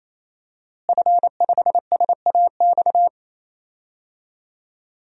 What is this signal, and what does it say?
Morse code, heard as a keyed tone, about 28 wpm, 700 Hz. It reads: F5SAX
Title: DXCC-100 Call Signs - 5 Characters - Repeated 3x - 28wpm